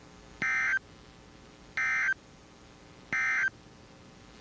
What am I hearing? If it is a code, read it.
Emergency Alert System (SAME) tones, heard as harsh buzzing bursts. It is heard from 0.3 to 3.6 s.
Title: NOAA Weather Radio WXJ97